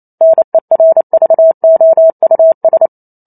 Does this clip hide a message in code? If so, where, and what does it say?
Yes: Morse code from 0.2 to 2.9 s, heard as a keyed tone, about 28 wpm, 650 Hz. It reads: NERVOUS